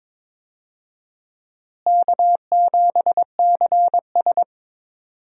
Morse code, heard as a keyed tone, about 22 wpm, 700 Hz. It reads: K7CS
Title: USA Call Signs - 4 Characters - Repeated 2x - 22wpm